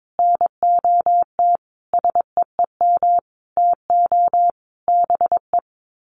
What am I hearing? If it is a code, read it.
Morse code, heard as a keyed tone, about 22 wpm, 700 Hz. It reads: NOT SEEM TO BE
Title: Sets of 5 Words 22wpm